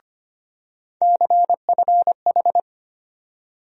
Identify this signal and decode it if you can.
Morse code, heard as a keyed tone, about 25 wpm, 700 Hz. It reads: CFH
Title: USA Call Sign Postfixes - 3 Letters 25wpm